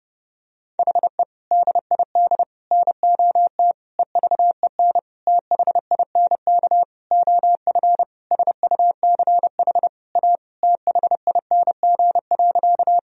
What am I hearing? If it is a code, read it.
Morse code, heard as a keyed tone, about 30 wpm, 700 Hz. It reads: HE DID NOT EVEN THINK OF SUCH A THING.
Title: Sentences from Top 1000 Words - Review Part 1 - 30wpm